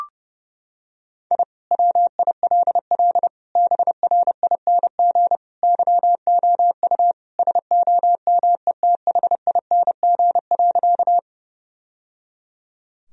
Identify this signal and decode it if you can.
Morse code, heard as a keyed tone, about 30 wpm, 700 Hz. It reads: I WILL BRING YOU SOMETHING.
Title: Sentences from Top 1000 Words - Review Part 3 - 30wpm